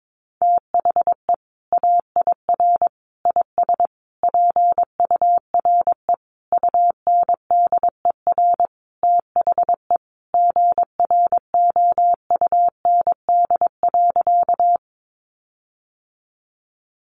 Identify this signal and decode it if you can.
Morse code, heard as a keyed tone, about 22 wpm, 700 Hz. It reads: THE AIR IS PURE UNDER THE GROUND.
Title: Sentences from Top 1500 Words 22wpm